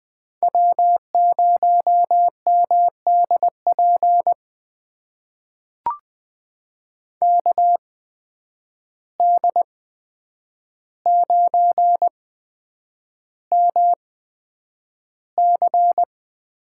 Morse code, heard as a keyed tone, about 20 wpm, spaced as for 9 wpm, 700 Hz. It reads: W0MDP KD9MC
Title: USA Call Signs - 5 Characters - 8x Character Spacing - 20wpm